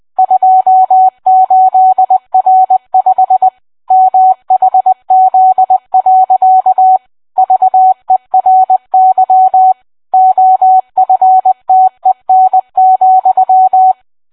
Morse code, heard as a keyed tone, about 20 wpm, 750 Hz. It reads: 28R5 MHZ. VERY OFTEN,